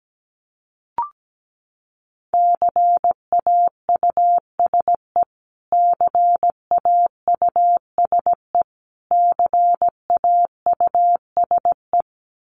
Morse code, heard as a keyed tone, about 17 wpm, 700 Hz. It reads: CAUSE CAUSE CAUSE
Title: Top 500 Words Repeated 17wpm